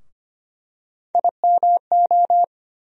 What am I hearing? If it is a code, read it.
Morse code, heard as a keyed tone, about 25 wpm, 700 Hz. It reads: IMO
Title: USA Call Sign Postfixes - 3 Letters 25wpm